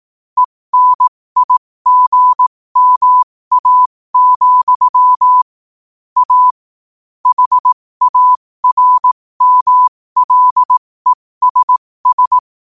Morse code, heard as a keyed tone, about 18 wpm, spaced as for 15 wpm, 1000 Hz. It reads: ENIGMA, A HARMLESS
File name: qotdfiles / 15wpm